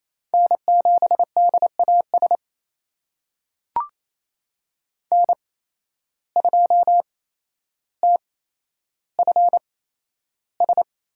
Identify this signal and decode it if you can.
Morse code, heard as a keyed tone, about 28 wpm, spaced as for 12 wpm, 700 Hz. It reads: N7DAS N2TFS
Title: USA Call Signs - 8x Character Spacing - 28wpm